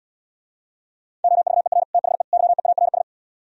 Morse code, heard as a keyed tone, about 77 wpm, 700 Hz. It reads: OVER THE YEARS